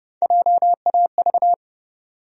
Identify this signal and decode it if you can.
Morse code, heard as a keyed tone, about 30 wpm, 700 Hz. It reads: JAV
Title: USA Call Sign Postfixes - 3 Letters 30wpm